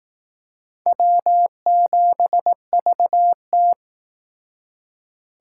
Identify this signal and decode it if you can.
Morse code, heard as a keyed tone, about 18 wpm, 700 Hz. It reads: W7VT
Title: Speed Racing - USA 1x2 Call Signs 15wpm